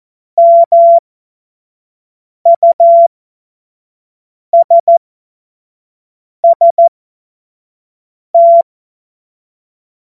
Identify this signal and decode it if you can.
Morse code, heard as a keyed tone, about 14 wpm, spaced as for 5 wpm, 680 Hz. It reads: MUSST
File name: Morsekurs / Witze / Witz 09. Dezember 2025 25 F70